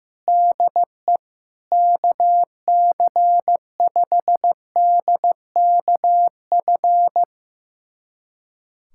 Morse code, heard as a keyed tone, about 15 wpm, 700 Hz. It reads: DE KC5DKF
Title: USA Callsign DE Callsign - 6 Characters - 15wpm